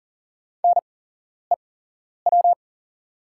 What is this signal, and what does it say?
Morse code, heard as a keyed tone, about 40 wpm, spaced as for 11 wpm, 690 Hz. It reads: NEW
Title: 8x Character Spacing - Top 500 Words - 40wpm